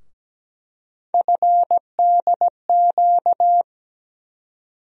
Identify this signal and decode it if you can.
Morse code, heard as a keyed tone, about 17 wpm, 700 Hz. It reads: FDQ
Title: USA Call Sign Postfixes - 3 Letters 17wpm